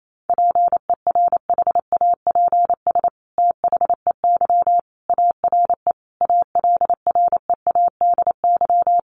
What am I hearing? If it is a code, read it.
Morse code, heard as a keyed tone, about 28 wpm, 700 Hz. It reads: PERHAPS THEY ARE ALREADY
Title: Sentences from Top 700 Words 28wpm